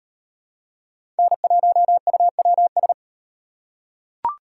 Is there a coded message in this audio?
Yes: Morse code from 1.2 to 2.9 s, heard as a keyed tone, about 35 wpm, 700 Hz. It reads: N1UWS